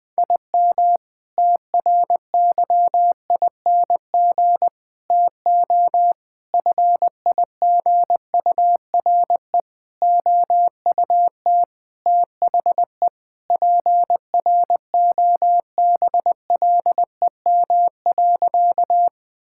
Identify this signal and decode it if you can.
Morse code, heard as a keyed tone, about 21 wpm, 700 Hz. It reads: IM TRYING TO FIGURE OUT THE PROBLEM.
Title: Sentences with 4-Word Phrases - 20wpm